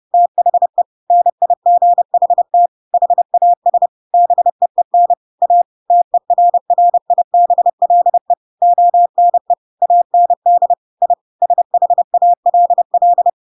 Morse code, heard as a keyed tone, about 30 wpm, 700 Hz. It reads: THE NIGHT HAS BEEN A TERRIBLE ONE AND I SHALL
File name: Pinocchio / Pinocchio ch7 30 30 ik1hge